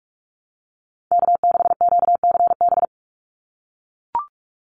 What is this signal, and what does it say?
Morse code, heard as a keyed tone, about 44 wpm, 700 Hz. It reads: K6QCB
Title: NA SOTA Chaser Callsigns 45wpm